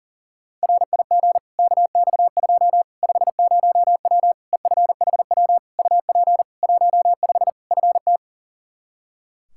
Morse code, heard as a keyed tone, about 37 wpm, 700 Hz. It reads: RIG KX2 50W EFHW UP 15 FT
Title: K1USN SST QSO Practice 40wpm